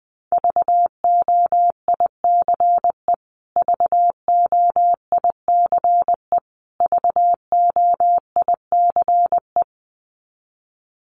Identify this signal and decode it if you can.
Morse code, heard as a keyed tone, about 20 wpm, 700 Hz. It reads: VOICE VOICE VOICE
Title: Top 500 Words Repeated 20wpm